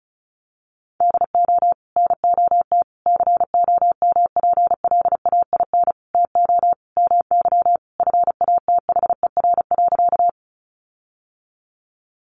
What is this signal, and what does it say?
Morse code, heard as a keyed tone, about 34 wpm, 700 Hz. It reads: DO NOT COMPLAIN TO MY FATHER.
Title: Sentences from Top 1300 Words 35wpm